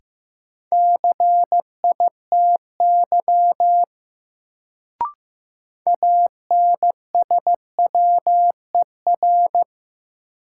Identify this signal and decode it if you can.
Morse code, heard as a keyed tone, about 15 wpm, 700 Hz. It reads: CITY ANSWER
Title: Sentences from Top 300 Words - New Word Drill - 15wpm